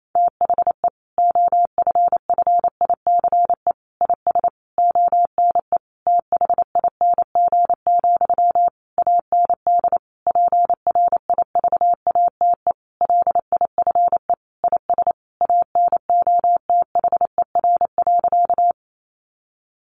Morse code, heard as a keyed tone, about 28 wpm, 700 Hz. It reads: THE OFFICE IS ONE THING, AND PRIVATE LIFE IS ANOTHER.
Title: Sentences from Top 1000 Words 28wpm